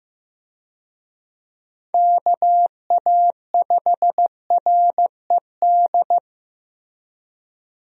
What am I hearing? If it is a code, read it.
Morse code, heard as a keyed tone, about 15 wpm, 700 Hz. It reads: KA5RED
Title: USA Call Signs - 6 Characters - Repeated 3x - 15wpm